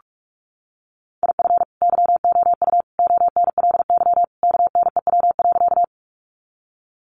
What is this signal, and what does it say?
Morse code, heard as a keyed tone, about 44 wpm, 700 Hz. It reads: IF YOU ONLY KNEW.
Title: Sentences from Top 400 Words 45wpm